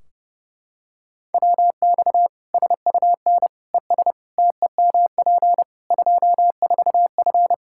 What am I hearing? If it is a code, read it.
Morse code, heard as a keyed tone, about 30 wpm, 700 Hz. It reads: WX SUN ES TEMP 24F
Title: Practice QSO - Exchange 2 - Info - 30wpm